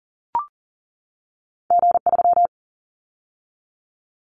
Morse code, heard as a keyed tone, about 42 wpm, 700 Hz. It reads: G3
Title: Callsign Prefixes for Top 25 DXCC - 40wpm